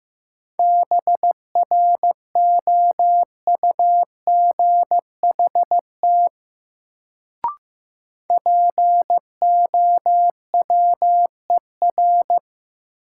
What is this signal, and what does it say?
Morse code, heard as a keyed tone, about 15 wpm, 700 Hz. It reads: BROUGHT POWER
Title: Sentences from Top 500 Words - New Word Drill - 15wpm